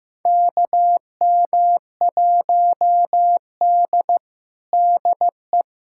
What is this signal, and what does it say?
Morse code, heard as a keyed tone, about 14 wpm, 700 Hz. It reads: KM1D DE
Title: USA Callsign DE Callsign - 4 Characters - 15wpm